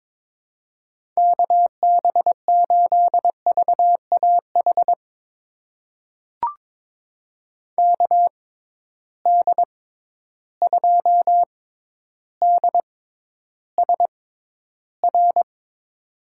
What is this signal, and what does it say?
Morse code, heard as a keyed tone, about 22 wpm, spaced as for 11 wpm, 700 Hz. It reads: KB8VAH KD2DSR
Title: USA Call Signs - 6 Characters - 6x Character Spacing - 22wpm